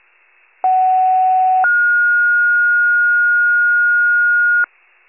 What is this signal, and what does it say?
Two-tone page, heard as a one-second tone, then a three-second tone, about 750 then 1430 Hz. It is heard from 0.6 to 4.6 s.